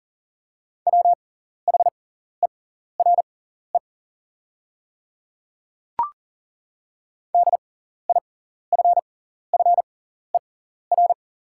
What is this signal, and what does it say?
Morse code, heard as a keyed tone, about 41 wpm, spaced as for 14 wpm, 700 Hz. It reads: WHERE DIFFER
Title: 6x Character Spacing - Top 500 Words - 40wpm